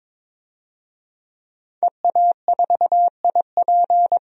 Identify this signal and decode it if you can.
Morse code, heard as a keyed tone, about 22 wpm, 700 Hz. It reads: EA4IP
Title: DXCC-100 Call Signs - 5 Characters - 22wpm